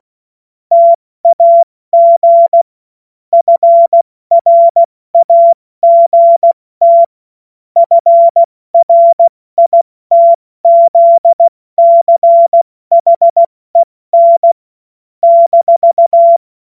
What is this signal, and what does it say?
Morse code, heard as a keyed tone, about 16 wpm, 680 Hz. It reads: TAG FRAGT FRITZCHEN -
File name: Morsekurs / Witze / Witz 21. September 2021, 70 F80